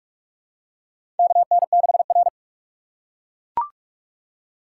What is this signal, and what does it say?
Morse code, heard as a keyed tone, about 41 wpm, 690 Hz. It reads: KN6R
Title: USA Call Signs - 4 Characters - 2x Character Spacing - 45wpm